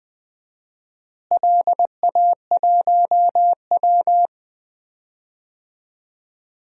Morse code, heard as a keyed tone, about 21 wpm, 700 Hz. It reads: LA1W
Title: DXCC-100 Call Signs - 4 Characters - Repeated 3x - 20wpm